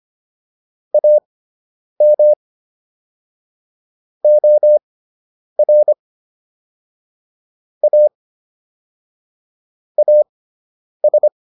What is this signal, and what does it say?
Morse code, heard as a keyed tone, about 25 wpm, spaced as for 9 wpm, 600 Hz. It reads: AM OR A AS